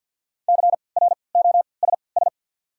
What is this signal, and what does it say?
Morse code, heard as a keyed tone, about 47 wpm, spaced as for 28 wpm, 700 Hz. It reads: CROSS
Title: 3x Character Spacing - Top 500 Words - 50wpm